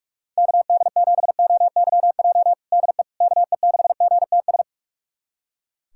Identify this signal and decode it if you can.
Morse code, heard as a keyed tone, about 45 wpm, 700 Hz. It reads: KD7OYJ DE KE6GTS
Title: USA Callsign DE Callsign - 6 Characters - 45wpm